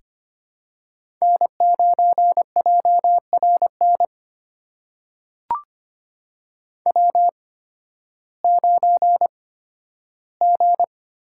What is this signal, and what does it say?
Morse code, heard as a keyed tone, about 25 wpm, spaced as for 13 wpm, 700 Hz. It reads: N9JRN W9G
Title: USA Call Signs - 5 Characters - 8x Character Spacing - 25wpm